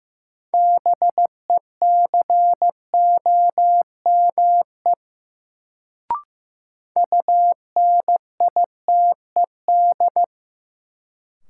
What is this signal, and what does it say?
Morse code, heard as a keyed tone, about 15 wpm, 700 Hz. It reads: BECOME UNITED
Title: Sentences from Top 600 Words - New Word Drill - 15wpm